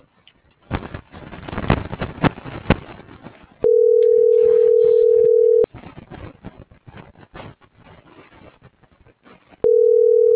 Ringback tone (North America), 440 + 480 Hz.
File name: CENTRAL GENERAL / out yarit.rdz - .94256